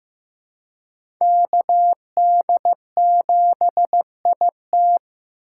Morse code, heard as a keyed tone, about 15 wpm, 700 Hz. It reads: KD7IT
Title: USA Call Signs - 5 Characters - 3x Character Spacing - 15wpm